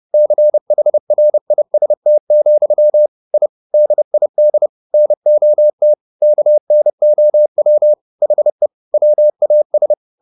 Morse code, heard as a keyed tone, about 30 wpm, 600 Hz. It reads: CHRIST, I DID NOT KNOW HE WAS
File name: the old man and the sea / the old man and the sea 1 30wpm